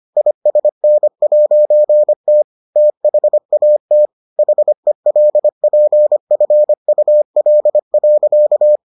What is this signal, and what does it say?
Morse code, heard as a keyed tone, about 25 wpm, 600 Hz. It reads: ISN'T THAT HELPFUL.